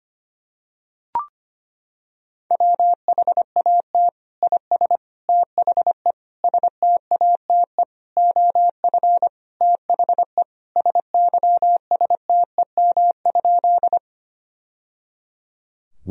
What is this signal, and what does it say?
Morse code, heard as a keyed tone, about 25 wpm, 700 Hz. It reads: WHAT IS THE STATE OF THE SYSTEM?
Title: Sentences from Top 1000 Words - Review Part 3 - 25wpm